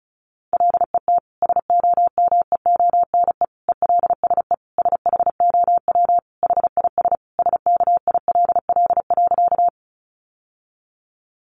Morse code, heard as a keyed tone, about 34 wpm, 700 Hz. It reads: LET SOMEONE ELSE SHOW HIS SKILL.
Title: Sentences from Top 1200 Words 35wpm